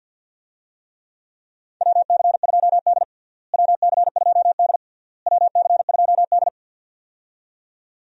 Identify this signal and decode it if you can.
Morse code, heard as a keyed tone, about 51 wpm, 700 Hz. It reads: WC2D WC2D WC2D
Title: USA Call Signs - 4 Characters - Repeated 3x - 50wpm